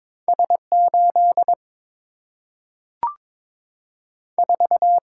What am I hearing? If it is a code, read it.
Morse code, heard as a keyed tone, about 22 wpm, 700 Hz. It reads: S 8 4
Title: Single-Letter Single-Number 22wpm